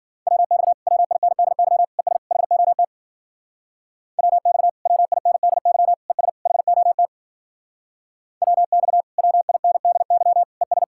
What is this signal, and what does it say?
Morse code, heard as a keyed tone, about 52 wpm, 700 Hz. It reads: WX WINDY ES HOT WX WINDY ES HOT WX WINDY ES